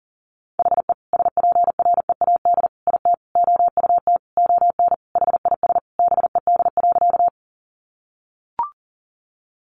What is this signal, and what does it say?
Morse code, heard as a keyed tone, about 40 wpm, 700 Hz. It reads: HE SPREAD IT OUT ON HIS BED.
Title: Sentences from Top 1200 Words 40wpm